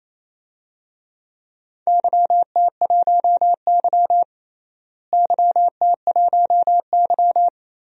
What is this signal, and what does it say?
Morse code, heard as a keyed tone, about 28 wpm, 700 Hz. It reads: YT1Y YT1Y
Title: DXCC-100 Call Signs - 4 Characters - Repeated 2x - 28wpm